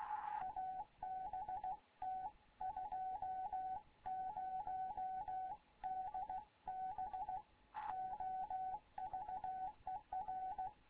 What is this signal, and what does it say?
Morse code, heard as a keyed tone, about 16 wpm, 750 Hz, with radio noise. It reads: ABT 20DB OVER
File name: Morze-HA2MN / hangp / 2 11 GYAKORLAT